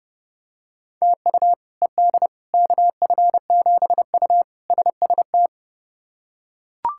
Morse code, heard as a keyed tone, about 32 wpm, 700 Hz. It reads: TU ED KF7U SST
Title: K1USN SST QSO Practice - 30wpm